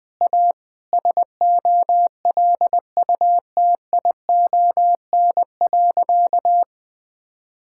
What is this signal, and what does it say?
Morse code, heard as a keyed tone, about 20 wpm, 700 Hz. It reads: A SOLUTION.
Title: Sentences with 4-Word Phrases - 20wpm